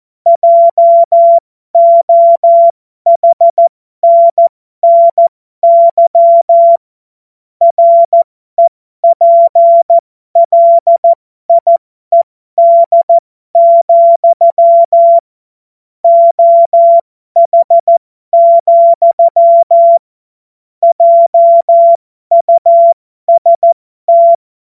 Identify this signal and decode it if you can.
Morse code, heard as a keyed tone, about 14 wpm, spaced as for 12 wpm, 680 Hz. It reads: JOHNNY REPLIED, OH, JUST
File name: Morsekurs / Witze